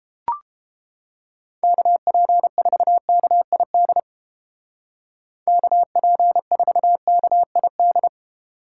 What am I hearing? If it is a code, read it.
Morse code, heard as a keyed tone, about 31 wpm, 700 Hz. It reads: KP4KID KP4KID